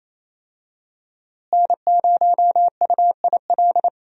Morse code, heard as a keyed tone, about 27 wpm, 700 Hz. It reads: N0UIL
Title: USA Call Signs - 5 Characters - Repeated 2x - 28wpm